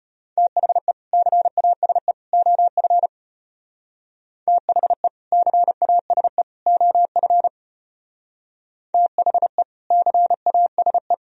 Morse code, heard as a keyed tone, about 31 wpm, 700 Hz. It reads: THE CASE OF THE CASE OF THE CASE
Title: Speed-Racing - Sets of 3 Words 25wpm